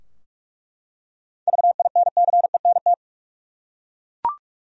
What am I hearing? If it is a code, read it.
Morse code, heard as a keyed tone, about 46 wpm, 700 Hz. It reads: VINCENT
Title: USA Common Names - 45wpm